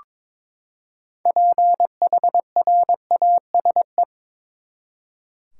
Morse code, heard as a keyed tone, about 22 wpm, 700 Hz. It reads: PHRASE
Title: Six Letter Words - 22wpm